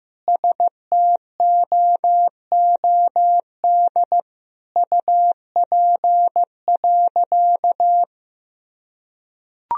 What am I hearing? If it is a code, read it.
Morse code, heard as a keyed tone, about 15 wpm, 700 Hz. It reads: STOOD UP.
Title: Sentences from Top 1000 Words - Review Part 2 - 15wpm